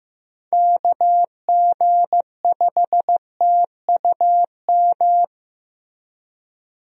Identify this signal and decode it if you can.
Morse code, heard as a keyed tone, about 16 wpm, 700 Hz. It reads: KG5TUM
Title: USA Call Signs - 6 Characters - Repeated 3x - 15wpm